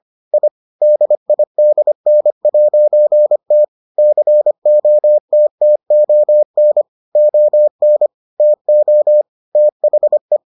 Morse code, heard as a keyed tone, about 25 wpm, 600 Hz. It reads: I DIDN'T COTTON ON TO THE